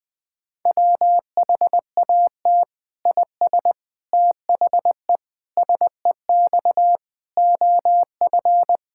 Morse code, heard as a keyed tone, about 21 wpm, 700 Hz. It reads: WHAT IS THE SEX OF
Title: Sentences from Top 1600 Words 20wpm